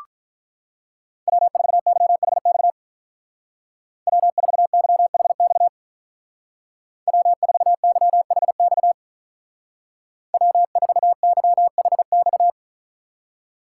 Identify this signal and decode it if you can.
Morse code, heard as a keyed tone, about 44 wpm, 700 Hz. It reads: W4YHX W4YHX W4YHX W4YHX
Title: Speed Racing - USA Call Signs 35wpm